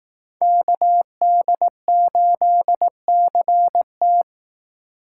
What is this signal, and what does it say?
Morse code, heard as a keyed tone, about 18 wpm, 700 Hz. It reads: KD8CT
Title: Speed Racing - USA 2x2 Call Signs 15wpm